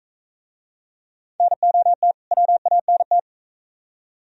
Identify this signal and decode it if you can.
Morse code, heard as a keyed tone, about 39 wpm, 700 Hz. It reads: NOT WANT